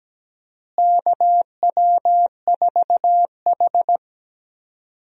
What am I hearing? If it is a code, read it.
Morse code, heard as a keyed tone, about 17 wpm, 700 Hz. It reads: KW4H
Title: USA Call Signs - 4 Characters - Repeated 3x - 17wpm